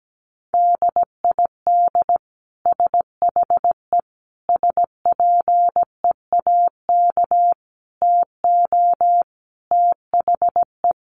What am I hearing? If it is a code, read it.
Morse code, heard as a keyed tone, about 17 wpm, 700 Hz. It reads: DID SHE SPEAK TO THE
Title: Sentences from Top 1200 Words 17wpm